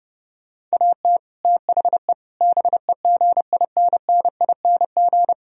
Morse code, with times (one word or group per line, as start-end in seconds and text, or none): 0.7-1.2 AT
1.4-2.1 THE
2.4-5.3 BEGINNING